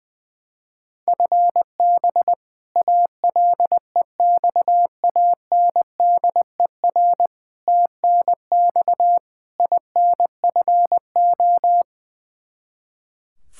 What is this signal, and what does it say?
Morse code, heard as a keyed tone, about 20 wpm, 700 Hz. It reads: FB ALEXANDER TNX INFO
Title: K1USN SST QSO Practice 20wpm